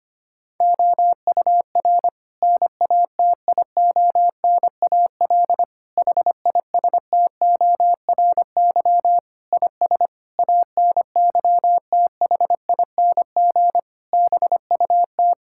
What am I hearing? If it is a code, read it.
Morse code, heard as a keyed tone, about 25 wpm, 700 Hz. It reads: OUR NATIONAL HISTORY IS ANYTHING BUT
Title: Sentences from Top 1000 Words - Review Part 6 - 25wpm